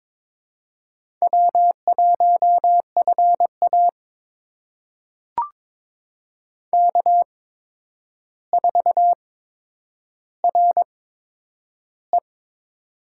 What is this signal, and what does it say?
Morse code, heard as a keyed tone, about 22 wpm, spaced as for 9 wpm, 700 Hz. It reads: W1FA K4RE
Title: USA Call Signs - 4 Characters - 8x Character Spacing - 22wpm